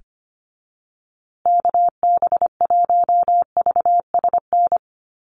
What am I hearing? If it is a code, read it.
Morse code, heard as a keyed tone, about 25 wpm, 700 Hz. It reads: KB1VSN
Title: USA Call Signs Repeated 2x 25wpm